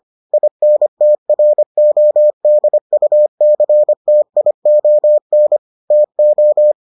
Morse code, heard as a keyed tone, about 25 wpm, 600 Hz. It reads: INTRODUCTION TO